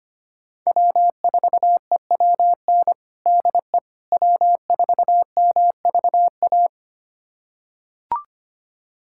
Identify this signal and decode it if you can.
Morse code, heard as a keyed tone, about 25 wpm, 700 Hz. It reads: W4EWN DE W4MVA
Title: USA Callsign DE Callsign - 5 Characters - 25wpm